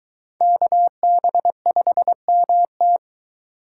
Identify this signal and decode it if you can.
Morse code, heard as a keyed tone, about 21 wpm, 700 Hz. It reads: KB5MT